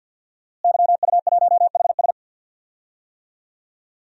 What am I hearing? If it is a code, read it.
Morse code, heard as a keyed tone, about 52 wpm, 700 Hz. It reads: YU1HS
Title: DXCC-100 Call Signs - 5 Characters - Repeated 2x - 50wpm